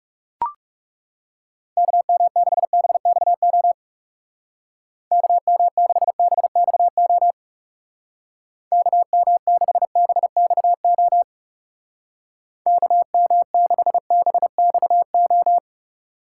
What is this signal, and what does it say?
Morse code, heard as a keyed tone, about 37 wpm, 700 Hz. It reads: KM6BXO KM6BXO KM6BXO KM6BXO
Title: Speed Racing - USA 2x3 Call Signs 30wpm